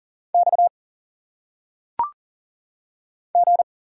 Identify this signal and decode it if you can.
Morse code, heard as a keyed tone, about 40 wpm, 700 Hz. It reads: X G
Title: Single Letters - Phonetic Pronunciation 40wpm